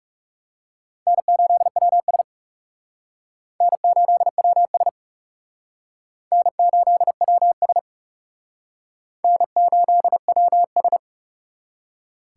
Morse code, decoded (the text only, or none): N8WS N8WS N8WS N8WS